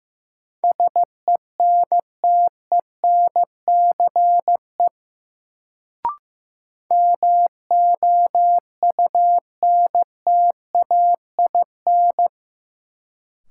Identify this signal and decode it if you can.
Morse code, heard as a keyed tone, about 15 wpm, 700 Hz. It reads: SENTENCE MOUNTAIN
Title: Sentences from Top 1000 Words - Review Part 1 - New Word Drill - 15wpm